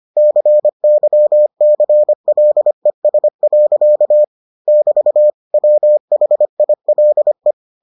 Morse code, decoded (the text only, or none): CYCLES. <BT> WHILE